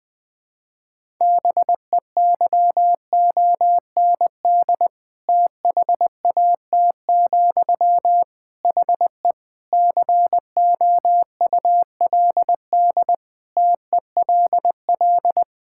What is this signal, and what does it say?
Morse code, heard as a keyed tone, about 21 wpm, 700 Hz. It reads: BEYOND THAT, HE COULD TELL
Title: Sentences from Top 1000 Words - Review Part 6 - 20wpm